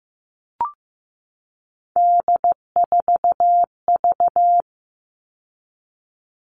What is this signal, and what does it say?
Morse code, heard as a keyed tone, about 15 wpm, 700 Hz. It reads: D4V
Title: All Cut Numbers - 3-Digit 15wpm